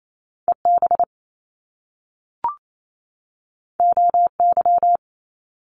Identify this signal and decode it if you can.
Morse code, heard as a keyed tone, about 28 wpm, 700 Hz. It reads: E B O Y
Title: Top Two Letter Combinations in Words 28wpm